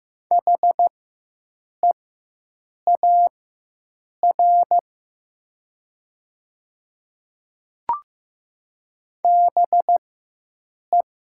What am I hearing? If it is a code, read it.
Morse code, heard as a keyed tone, about 15 wpm, spaced as for 7 wpm, 700 Hz. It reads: HEAR BE
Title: 4x Character Spacing - Top 300 Words - 15wpm